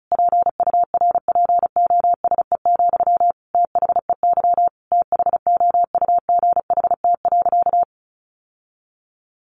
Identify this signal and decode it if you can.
Morse code, heard as a keyed tone, about 34 wpm, 700 Hz. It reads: PURPOSE, THEY THOUGHT.
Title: Sentences from Top 800 Words 35wpm